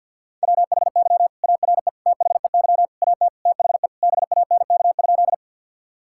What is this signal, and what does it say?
Morse code, heard as a keyed tone, about 50 wpm, 700 Hz. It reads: WHY ARE THEY AT THE BANK?
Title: Sentences from Top 1000 Words - Review Part 6 - 50wpm